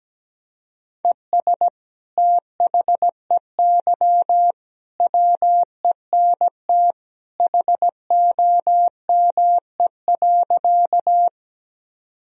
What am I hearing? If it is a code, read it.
Morse code, heard as a keyed tone, about 17 wpm, 700 Hz. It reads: ES THEY WENT HOME.
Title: Sentences from Top 200 Words and Abbreviations 17wpm